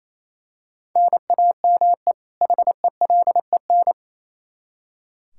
Morse code, heard as a keyed tone, about 28 wpm, 700 Hz. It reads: NAME HELEN
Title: Practice QSO - Exchange 1 - Beginning - 28wpm